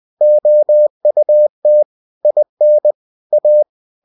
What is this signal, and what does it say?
Morse code, heard as a keyed tone, about 20 wpm, 600 Hz. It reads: OUT IN A
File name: cw-books / winnie-the-pooh / winnie-20wpm-page-0029